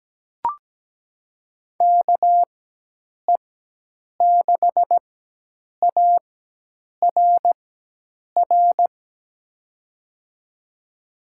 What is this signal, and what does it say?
Morse code, heard as a keyed tone, about 17 wpm, spaced as for 8 wpm, 700 Hz. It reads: KE6ARR
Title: USA Call Signs - 6 Characters - 4x Character Spacing - 17wpm